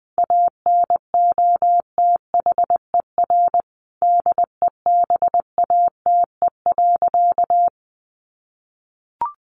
Morse code, heard as a keyed tone, about 20 wpm, 700 Hz. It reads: ANOTHER DEBATE.
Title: Sentences from Top 1100 Words 20wpm